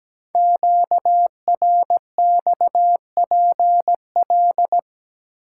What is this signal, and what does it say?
Morse code, heard as a keyed tone, about 17 wpm, 700 Hz. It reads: QRXPL
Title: Code Groups - 5 Letters - 17wpm